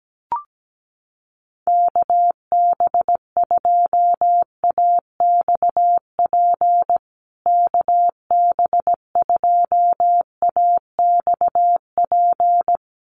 Morse code, heard as a keyed tone, about 17 wpm, 700 Hz. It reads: KB2AXP KB2AXP
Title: USA Call Signs Repeated 2x 17wpm